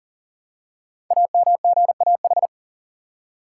Morse code, heard as a keyed tone, about 39 wpm, 690 Hz. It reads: AMGAH
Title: Code Groups - 5 Letters - 40wpm